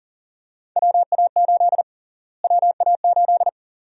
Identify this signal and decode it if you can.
Morse code, heard as a keyed tone, about 40 wpm, 700 Hz. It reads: WA8 WA8
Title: USA Call Sign Prefixes - Repeated 2x - 40wpm